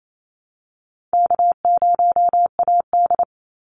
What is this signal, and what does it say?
Morse code, heard as a keyed tone, about 28 wpm, 700 Hz. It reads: K0AD
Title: Easy Call Signs Repeated 2x - 28wpm